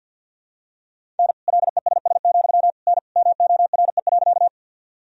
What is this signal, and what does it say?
Morse code, heard as a keyed tone, about 47 wpm, 700 Hz. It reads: N LESS, N MORE.
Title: Sentences from Top 400 Words and Abbreviations 50wpm